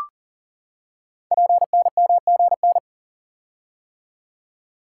Morse code, heard as a keyed tone, about 42 wpm, 700 Hz. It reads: PNMGN
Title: Code Groups - 5 Letters - 40wpm